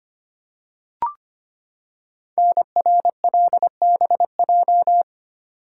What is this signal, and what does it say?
Morse code, heard as a keyed tone, about 25 wpm, 700 Hz. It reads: NRLBJ
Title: Code Groups - 5 Letters - 25wpm